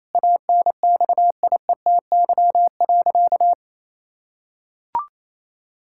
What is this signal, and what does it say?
Morse code, heard as a keyed tone, about 28 wpm, 700 Hz. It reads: ANXIETY.
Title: Sentences from Top 1600 Words 28wpm